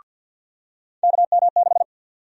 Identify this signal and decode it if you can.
Morse code, heard as a keyed tone, about 52 wpm, 700 Hz. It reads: KM6
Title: USA Call Sign Prefixes - 50wpm